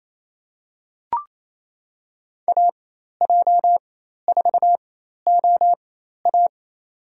Morse code, heard as a keyed tone, about 28 wpm, spaced as for 13 wpm, 700 Hz. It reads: AJ4OA
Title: USA Call Signs - 5 Characters - 4x Character Spacing - 28wpm